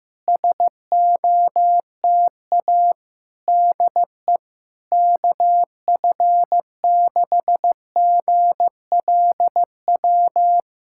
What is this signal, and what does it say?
Morse code, heard as a keyed tone, about 15 wpm, 700 Hz. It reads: SOTA DE KF6GLW
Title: SOTA QSO Practice 15wpm